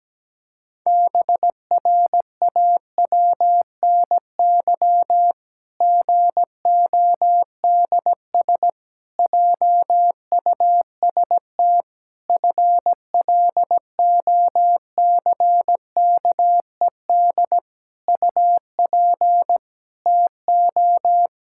Morse code, heard as a keyed tone, about 17 wpm, 700 Hz. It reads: BRAWNY GODS JUST FLOCKED UP TO
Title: Pangram Challenge v2 - 17wpm